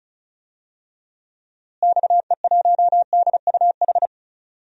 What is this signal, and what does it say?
Morse code, heard as a keyed tone, about 34 wpm, 700 Hz. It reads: XE1DUH
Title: DXCC-100 Call Signs - 6 Characters - Repeated 2x - 35wpm